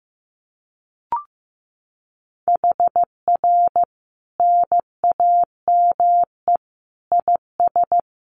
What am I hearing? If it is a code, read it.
Morse code, heard as a keyed tone, about 15 wpm, 700 Hz. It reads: HR NAME IS
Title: Common QSO Phrases 15wpm